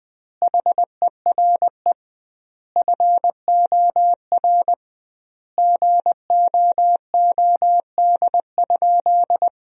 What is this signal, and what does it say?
Morse code, heard as a keyed tone, about 20 wpm, 700 Hz. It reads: HERE FOR GOOD?
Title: 2x Word-Spacing - Sentences from Top 200 Words - 20wpm